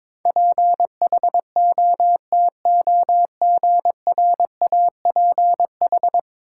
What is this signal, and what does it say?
Morse code, heard as a keyed tone, about 21 wpm, 700 Hz. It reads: PHOTOGRAPH
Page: morse-code-ninja-practice-22wpm